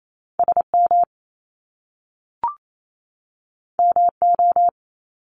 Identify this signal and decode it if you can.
Morse code, heard as a keyed tone, about 28 wpm, 700 Hz. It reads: S M M O
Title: Top Two Letter Combinations in Words 28wpm